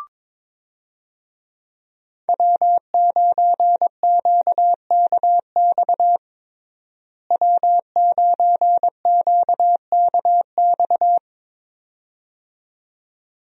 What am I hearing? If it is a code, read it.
Morse code, heard as a keyed tone, about 22 wpm, 700 Hz. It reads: W9QKX W9QKX
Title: USA Call Signs - 5 Characters - Repeated 2x - 22wpm